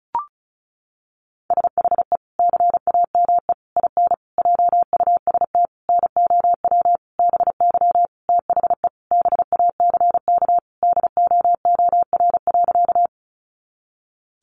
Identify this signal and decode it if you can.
Morse code, heard as a keyed tone, about 34 wpm, 700 Hz. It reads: SHE CAME IN JUST NOW BY THE BACK DOOR.
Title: Sentences from Top 300 Words 35wpm